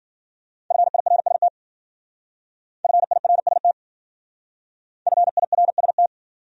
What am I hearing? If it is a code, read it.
Morse code, heard as a keyed tone, about 52 wpm, 700 Hz. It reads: FIRST FIRST FIRST